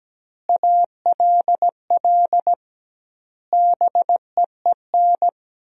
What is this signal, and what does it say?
Morse code, heard as a keyed tone, about 17 wpm, 700 Hz. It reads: ALL BEEN
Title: 2x Word-Spacing - Sentences from Top 200 Words - 17wpm